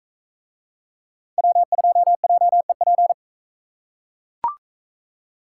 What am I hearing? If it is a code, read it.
Morse code, heard as a keyed tone, about 39 wpm, 700 Hz. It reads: W2JEP